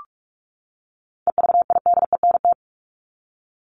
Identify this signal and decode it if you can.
Morse code, heard as a keyed tone, about 44 wpm, 700 Hz. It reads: EVIDENT
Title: LY Suffix - Top 100 Words with Base Words 45wpm